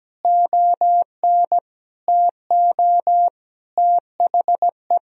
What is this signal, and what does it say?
Morse code, heard as a keyed tone, about 17 wpm, 700 Hz. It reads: ON TO THE
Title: Sentences from Top 1000 Words - Review Part 1 - 17wpm